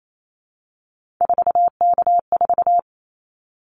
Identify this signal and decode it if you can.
Morse code, heard as a keyed tone, about 28 wpm, 700 Hz. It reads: 4K4
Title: Callsign Prefixes for Top 100 DXCC - 28wpm